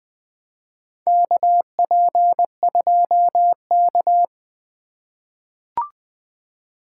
Morse code, heard as a keyed tone, about 20 wpm, 700 Hz. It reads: KP2K
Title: USA Call Signs - 4 Characters - 4x Character Spacing - 20wpm